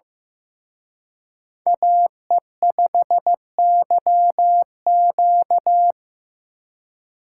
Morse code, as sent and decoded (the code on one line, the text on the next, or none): .- . ..... -.-- --.-
AE5YQ